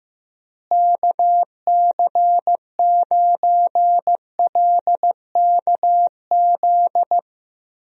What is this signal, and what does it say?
Morse code, heard as a keyed tone, about 15 wpm, 700 Hz. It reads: KC9LKZ
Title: USA Call Signs - 6 Characters - Repeated 2x - 15wpm